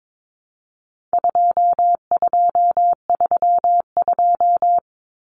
Morse code, heard as a keyed tone, about 22 wpm, 700 Hz. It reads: 2232
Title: Common Cut Numbers - 4-Digit 22wpm